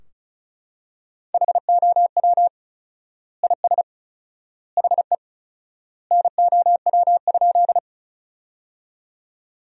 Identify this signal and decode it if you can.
Morse code, heard as a keyed tone, about 36 wpm, 700 Hz. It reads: HOW IS HE NOW?
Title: 4x Word-Spacing - Sentences from Top 100 Words - 35wpm